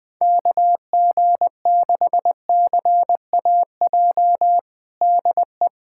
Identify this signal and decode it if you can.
Morse code, heard as a keyed tone, about 21 wpm, 700 Hz. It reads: KG6CAJ DE
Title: USA Callsign DE Callsign - 6 Characters - 20wpm